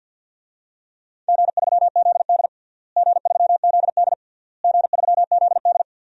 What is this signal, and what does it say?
Morse code, heard as a keyed tone, about 51 wpm, 700 Hz. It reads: G3ZD G3ZD G3ZD
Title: DXCC-100 Call Signs - 4 Characters - Repeated 3x - 50wpm